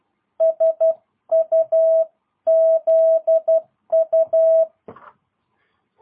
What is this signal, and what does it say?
Morse code, heard as a keyed tone, about 12 wpm, 660 Hz. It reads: SUZU
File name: jh8cbh / 2022cw